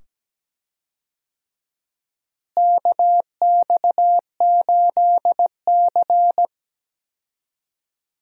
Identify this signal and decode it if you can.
Morse code, heard as a keyed tone, about 17 wpm, 700 Hz. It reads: KX8C
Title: USA Call Signs - 4 Characters - Repeated 2x - 17wpm